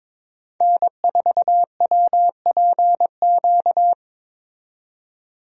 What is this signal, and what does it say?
Morse code, heard as a keyed tone, about 21 wpm, 700 Hz. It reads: N4WPQ
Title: USA Call Signs - 5 Characters - 22wpm